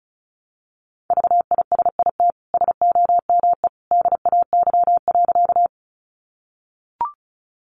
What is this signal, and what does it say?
Morse code, heard as a keyed tone, about 34 wpm, 700 Hz. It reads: VISIT SOME DAY.
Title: Sentences from Top 900 Words 35wpm